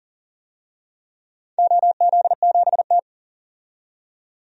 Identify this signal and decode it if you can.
Morse code, heard as a keyed tone, about 38 wpm, 700 Hz. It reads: OZ7T
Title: DXCC-100 Call Signs - 4 Characters - 40wpm